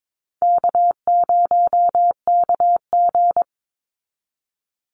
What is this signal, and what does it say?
Morse code, heard as a keyed tone, about 22 wpm, 700 Hz. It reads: K0KG
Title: Easy Call Signs Repeated 2x - 22wpm